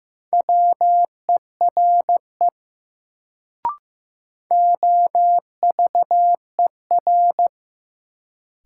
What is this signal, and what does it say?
Morse code, heard as a keyed tone, about 15 wpm, 700 Hz. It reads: WERE OVER
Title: Sentences from Top 100 Words - New Word Drill - 15wpm